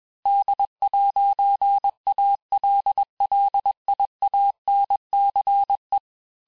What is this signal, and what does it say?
Morse code, heard as a keyed tone, about 21 wpm, 800 Hz. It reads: D'ALLIANCE
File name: texte-clair-vitesse-15mpm-puis-20mpm